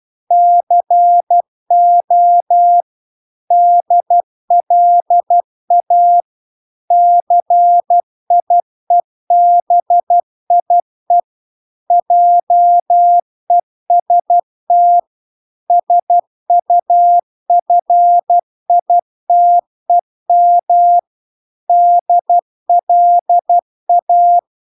Morse code, heard as a keyed tone, about 12 wpm, 700 Hz. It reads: CO DLA CIEBIE JEST SUFITEM DLA